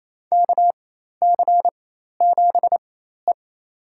Morse code, heard as a keyed tone, about 28 wpm, spaced as for 13 wpm, 700 Hz. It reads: KC7E
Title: USA Call Signs - 4 Characters - 4x Character Spacing - 28wpm